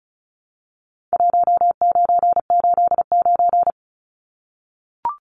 Morse code, heard as a keyed tone, about 34 wpm, 700 Hz. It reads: 1989
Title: Numbers - 4-Digit 35wpm